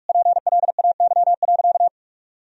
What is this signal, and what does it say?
Morse code, heard as a keyed tone, about 41 wpm, 700 Hz. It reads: PLAY.
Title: Sentences from Top 1000 Words - Review Part 4 - 45wpm